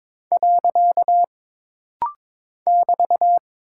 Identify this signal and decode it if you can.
Morse code, heard as a keyed tone, about 22 wpm, 700 Hz. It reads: . <BT>
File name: practice / 22wpm / Single Punctuation-Prosign - Rapid-Fire - 22wpm